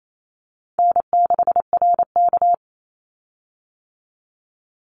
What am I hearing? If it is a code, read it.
Morse code, heard as a keyed tone, about 28 wpm, 700 Hz. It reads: N6RK
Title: Easy Call Signs - 28wpm